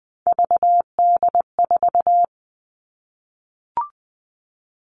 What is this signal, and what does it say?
Morse code, heard as a keyed tone, about 20 wpm, 700 Hz. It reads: VD4
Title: All Cut Numbers - 3-Digit 20wpm